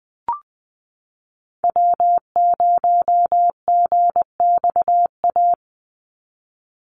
Morse code, heard as a keyed tone, about 20 wpm, 700 Hz. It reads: W0GXA
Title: Call Signs Repeated 2x - 20wpm